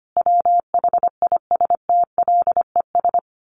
Morse code, heard as a keyed tone, about 25 wpm, 700 Hz. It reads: WHISTLES
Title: A Sound Of Thunder 25wpm